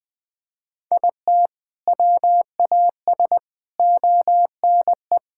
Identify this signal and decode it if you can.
Morse code, heard as a keyed tone, about 21 wpm, 700 Hz. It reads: IT WAS ONE
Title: Sentences with 5-Word Phrases - Spoken Beginning - 20wpm